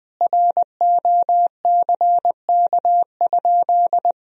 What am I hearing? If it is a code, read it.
Morse code, heard as a keyed tone, about 20 wpm, 700 Hz. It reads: ROCK?
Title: Sentences from Top 1000 Words - Review Part 2 - 20wpm